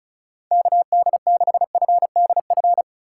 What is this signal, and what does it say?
Morse code, heard as a keyed tone, about 36 wpm, 700 Hz. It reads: KD6FDF
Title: USA Call Signs - 6 Characters - 2x Character Spacing - 35wpm